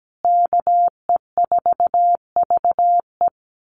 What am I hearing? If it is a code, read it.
Morse code, heard as a keyed tone, about 17 wpm, 700 Hz. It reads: KE4VE
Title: USA Call Signs Repeated 3x 17wpm